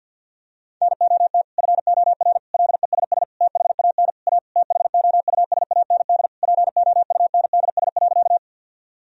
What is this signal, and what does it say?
Morse code, heard as a keyed tone, about 50 wpm, 700 Hz. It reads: NOT FOR LESS THAN A THOUSAND POUNDS.
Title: Sentences from Top 1000 Words - Review Part 3 - 50wpm